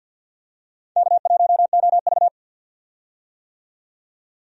Morse code, heard as a keyed tone, about 52 wpm, 700 Hz. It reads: K1OV
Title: USA Call Signs - 4 Characters - 50wpm